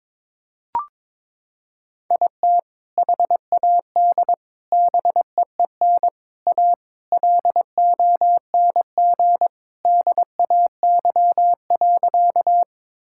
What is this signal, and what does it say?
Morse code, heard as a keyed tone, about 22 wpm, 700 Hz. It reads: IT HAD BEEN A LONG DAY.
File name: practice / 22wpm / Sentences with 3-Word Phrases - 22wpm